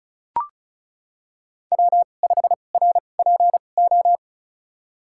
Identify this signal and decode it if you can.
Morse code, heard as a keyed tone, about 36 wpm, spaced as for 25 wpm, 700 Hz. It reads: W5RPO
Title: USA Call Signs - 5 Characters - 2x Character Spacing - 35wpm